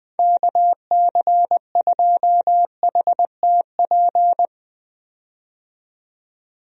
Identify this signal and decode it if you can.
Morse code, heard as a keyed tone, about 21 wpm, 700 Hz. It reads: KC2HTP
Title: USA Call Signs - 6 Characters - 20wpm